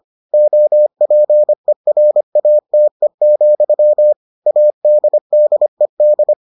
Morse code, heard as a keyed tone, about 25 wpm, 600 Hz. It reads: OPERATE, ADDED